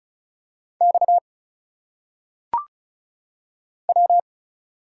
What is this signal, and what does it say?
Morse code, heard as a keyed tone, about 36 wpm, 700 Hz. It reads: X W
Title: Single Letters - Phonetic Pronunciation 35wpm